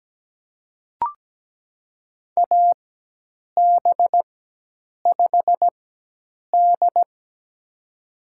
Morse code, heard as a keyed tone, about 17 wpm, spaced as for 8 wpm, 700 Hz. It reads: AB5D
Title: USA Call Signs - 4 Characters - 4x Character Spacing - 17wpm